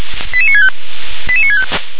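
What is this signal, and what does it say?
Five-tone selective call (ZVEI), heard as quick runs of five tones, tones of about 70 ms. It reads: 88965 88965
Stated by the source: ZVEI five-tone call received on SDR.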